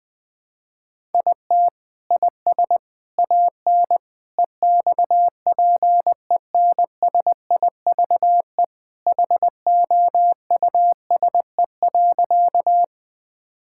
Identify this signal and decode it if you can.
Morse code, heard as a keyed tone, about 21 wpm, 700 Hz. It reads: IT IS AN EXPENSIVE HOUSE.
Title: Sentences from Top 1000 Words - Review Part 6 - 20wpm